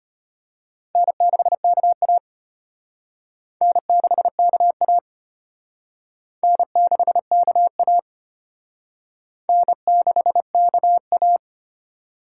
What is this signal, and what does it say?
Morse code, heard as a keyed tone, about 34 wpm, 700 Hz. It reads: N6KA N6KA N6KA N6KA